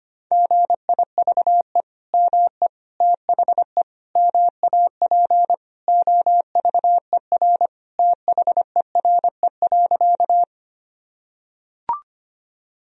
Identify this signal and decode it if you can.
Morse code, heard as a keyed tone, about 25 wpm, 700 Hz. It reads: GIVE ME THE MAP OVER THERE.
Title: Sentences from Top 1000 Words - Review Part 2 - 25wpm